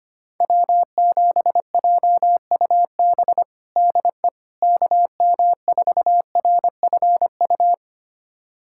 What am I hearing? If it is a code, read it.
Morse code, heard as a keyed tone, about 25 wpm, 700 Hz. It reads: W7JUB DE KM4RFU
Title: USA Callsign DE Callsign 25wpm